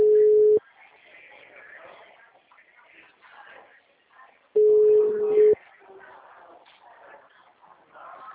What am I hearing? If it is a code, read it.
Ringback tone (Europe, 425 Hz).
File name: out- -2005